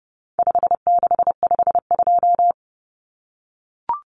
Morse code, heard as a keyed tone, about 30 wpm, 700 Hz. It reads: 5652
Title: Common Cut Numbers - 4-Digit 30wpm